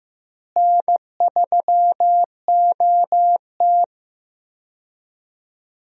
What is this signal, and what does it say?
Morse code, heard as a keyed tone, about 15 wpm, 700 Hz. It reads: N3OT
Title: USA Call Signs - 4 Characters - Repeated 3x - 15wpm